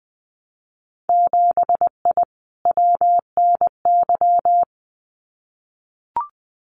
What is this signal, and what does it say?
Morse code, heard as a keyed tone, about 20 wpm, 700 Hz. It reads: 7I WNY
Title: Winter Field Day Exchanges Encore 20wpm